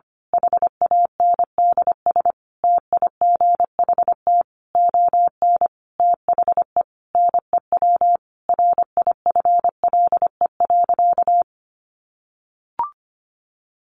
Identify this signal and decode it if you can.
Morse code, heard as a keyed tone, about 25 wpm, 700 Hz. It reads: HANDS TIGHT ON THE NEW RIFLE.
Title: A Sound Of Thunder 25wpm